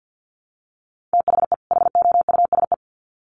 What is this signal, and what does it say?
Morse code, heard as a keyed tone, about 50 wpm, 700 Hz. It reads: THE HOUSE
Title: Sets of 2 Words 50wpm